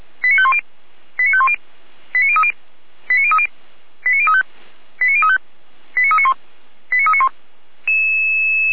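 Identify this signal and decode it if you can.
Five-tone selective call (ZVEI), heard as quick runs of five tones, tones of about 70 ms. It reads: 89510 89510 89030 89030 89035 89035 89391 89391
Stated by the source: ZVEI five-tone selective call received on SDR, logged by feuerwehr-orlamuende.org (Feuerwehr Orlamünde fire brigade).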